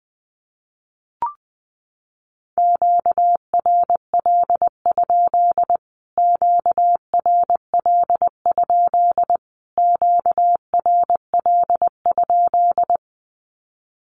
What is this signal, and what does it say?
Morse code, heard as a keyed tone, about 20 wpm, 700 Hz. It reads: QRL? QRL? QRL?
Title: Q-Codes Repeated 3x 20wpm